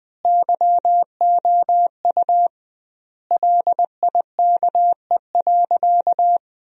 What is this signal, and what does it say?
Morse code, heard as a keyed tone, about 21 wpm, 700 Hz. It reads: YOU LIKE.
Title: 2x Word-Spacing - Sentences from Top 500 Words - 20wpm